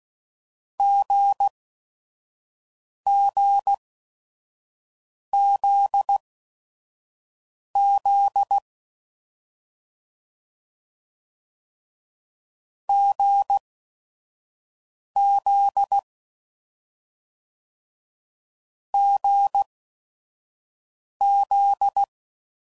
Morse code, heard as a keyed tone, about 16 wpm, spaced as for 5 wpm, 800 Hz. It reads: GGZZ GZ GZ
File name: Combinations / Difficult Combos / GZ GZ7